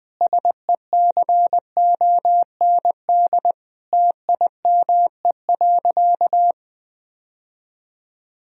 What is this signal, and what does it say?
Morse code, heard as a keyed tone, about 20 wpm, 700 Hz. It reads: SECOND TIME.
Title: Sentences from Top 1000 Words - Review Part 1 - 20wpm